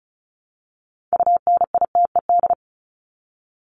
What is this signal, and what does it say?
Morse code, heard as a keyed tone, about 34 wpm, 700 Hz. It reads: UNITED
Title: ED Suffix - Top 100 Words with Base Words 35wpm